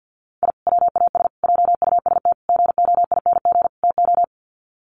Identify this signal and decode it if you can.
Morse code, heard as a keyed tone, about 50 wpm, 700 Hz. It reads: I WAS JUST GOING TO
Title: Sets of 5 Words Encore 50wpm